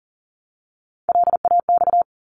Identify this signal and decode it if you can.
Morse code, heard as a keyed tone, about 42 wpm, 700 Hz. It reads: LAX
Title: ARRL Sections 40wpm